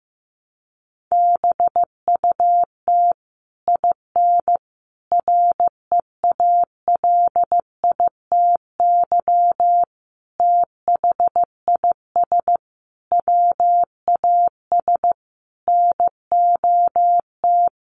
Morse code, heard as a keyed tone, about 15 wpm, 700 Hz. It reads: BUT IN REALITY THIS WAS NOT
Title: Sentences from Top 1200 Words 15wpm